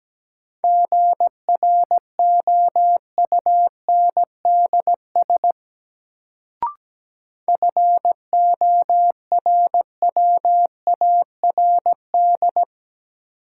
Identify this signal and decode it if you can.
Morse code, heard as a keyed tone, about 17 wpm, 700 Hz. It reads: GROUNDS FORWARD
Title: Sentences from Top 1300 Words - New Word Drill - 17wpm